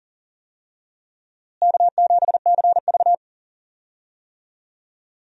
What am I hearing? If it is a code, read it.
Morse code, heard as a keyed tone, about 38 wpm, 700 Hz. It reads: K7CV
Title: USA Call Signs - 4 Characters - Repeated 3x - 40wpm